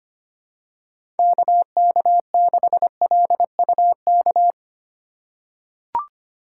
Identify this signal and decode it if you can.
Morse code, heard as a keyed tone, about 25 wpm, 700 Hz. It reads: KK6LUK
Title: USA Call Signs - 6 Characters - 6x Character Spacing - 25wpm